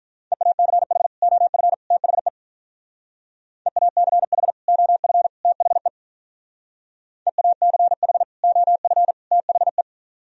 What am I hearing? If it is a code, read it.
Morse code, heard as a keyed tone, about 44 wpm, 700 Hz. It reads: EACH OF THE EACH OF THE EACH OF THE